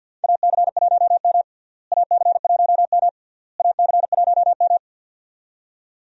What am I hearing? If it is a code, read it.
Morse code, heard as a keyed tone, about 49 wpm, 700 Hz. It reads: AC1M AC1M AC1M
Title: USA Call Signs - 4 Characters - Repeated 3x - 50wpm